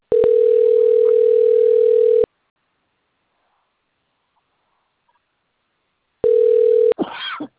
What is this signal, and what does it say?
Ringback tone (North America), 440 + 480 Hz.
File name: out- - - - - .13982